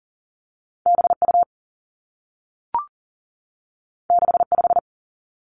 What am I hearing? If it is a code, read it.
Morse code, heard as a keyed tone, about 42 wpm, 700 Hz. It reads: B U 6 5
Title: Two Challenging Letter-Number Combinations 40wpm